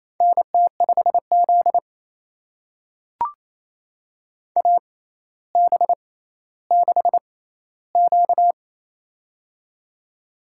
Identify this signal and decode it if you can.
Morse code, heard as a keyed tone, about 28 wpm, spaced as for 14 wpm, 700 Hz. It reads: NT5Z AB6Q
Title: USA Call Signs - 4 Characters - 6x Character Spacing - 28wpm